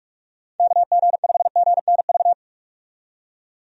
Morse code, heard as a keyed tone, about 42 wpm, 700 Hz. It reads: KG5GNV
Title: USA Call Signs - 6 Characters - 45wpm